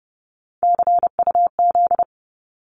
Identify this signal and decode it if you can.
Morse code, heard as a keyed tone, about 30 wpm, 700 Hz. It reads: CUZ
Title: Abbreviations 30wpm